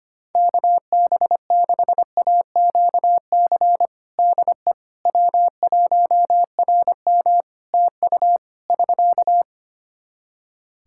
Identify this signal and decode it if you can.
Morse code, heard as a keyed tone, about 25 wpm, 700 Hz. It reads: KB6AQC DE W1RM TU <SK>
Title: Practice QSO - Exchange 4 - Ending - 25wpm